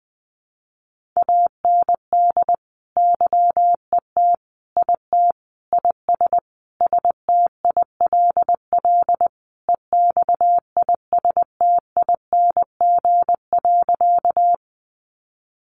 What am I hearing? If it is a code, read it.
Morse code, heard as a keyed tone, about 20 wpm, 700 Hz. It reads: AND YET IT IS STILL EXISTING.
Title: Sentences from Top 1500 Words 20wpm